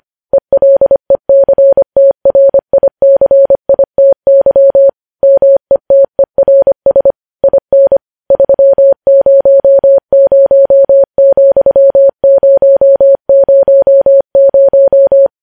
Morse code, heard as a keyed tone, about 25 wpm, 550 Hz. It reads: ELECTRICITY METERS IN 300,000
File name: podcast / News headlines-1x-25x25